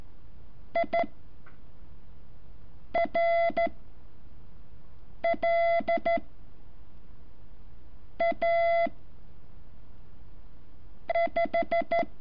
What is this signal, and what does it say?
Morse code, heard as a keyed tone, about 13 wpm, 680 Hz. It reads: I R L A 5
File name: curso / ejercicio 14